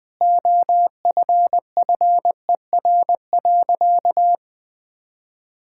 Morse code, heard as a keyed tone, about 20 wpm, 700 Hz. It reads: OFFER.
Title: Sentences from Top 1600 Words 20wpm